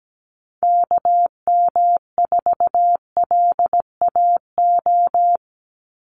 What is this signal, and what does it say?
Morse code, heard as a keyed tone, about 17 wpm, 700 Hz. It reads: KM4LAO
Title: Call Signs Repeated 3x - 17wpm